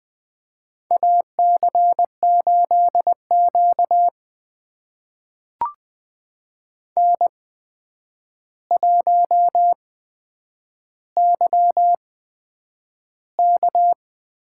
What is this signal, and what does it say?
Morse code, heard as a keyed tone, about 20 wpm, spaced as for 9 wpm, 700 Hz. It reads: AC8Q N1YK
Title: USA Call Signs - 8x Character Spacing - 20wpm